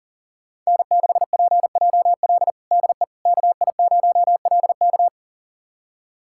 Morse code, heard as a keyed tone, about 40 wpm, 700 Hz. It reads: N6PJL DE KI0LK
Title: USA Callsign DE Callsign - 5 Characters - 40wpm